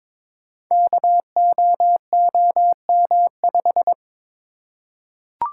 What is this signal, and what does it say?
Morse code, heard as a keyed tone, about 22 wpm, 690 Hz. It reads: KOOM5
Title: Code Groups - 5 Letter:Numbers - 22wpm